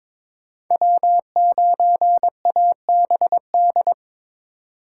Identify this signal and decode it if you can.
Morse code, heard as a keyed tone, about 26 wpm, 700 Hz. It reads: W9ABD
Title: Speed Racing - USA 1x3 Call Signs 22wpm